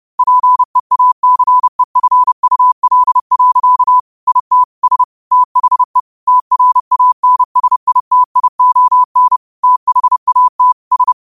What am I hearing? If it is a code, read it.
Morse code, heard as a keyed tone, about 30 wpm, 1000 Hz. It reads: PEACEFUL. IT S THE TRANSITION THAT S